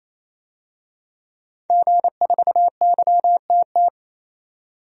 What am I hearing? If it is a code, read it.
Morse code, heard as a keyed tone, about 28 wpm, 700 Hz. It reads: G4YTT
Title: DXCC-100 Call Signs - 5 Characters - 28wpm